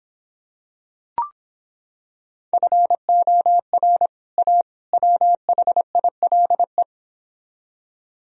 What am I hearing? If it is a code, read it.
Morse code, heard as a keyed tone, about 26 wpm, 700 Hz. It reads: FOR A WHILE